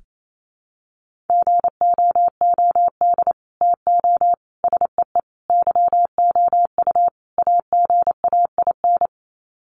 Morse code, heard as a keyed tone, about 28 wpm, 700 Hz. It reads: GOOD TO SEE YOU AGAIN
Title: Sets of 5 Words Encore2 28wpm